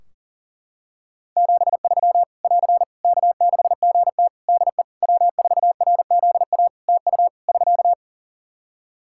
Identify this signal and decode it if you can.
Morse code, heard as a keyed tone, about 40 wpm, 700 Hz. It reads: 73 <AR> K6GT DE W4RZA TU <SK>
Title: Practice QSO - Exchange 4 - Ending - 40wpm